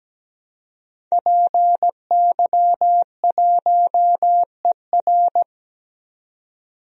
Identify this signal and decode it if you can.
Morse code, heard as a keyed tone, about 17 wpm, 700 Hz. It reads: PY1ER
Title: DXCC-100 Call Signs - 5 Characters - 17wpm